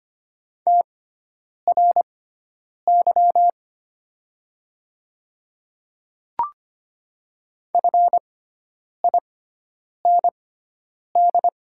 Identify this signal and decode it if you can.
Morse code, heard as a keyed tone, about 25 wpm, spaced as for 9 wpm, 700 Hz. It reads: TRY FIND
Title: 6x Character Spacing - Top 300 Words - 25wpm